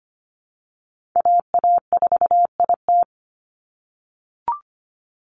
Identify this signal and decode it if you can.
Morse code, heard as a keyed tone, about 25 wpm, 700 Hz. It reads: AA4IT
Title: NA SOTA Chaser Callsigns 25wpm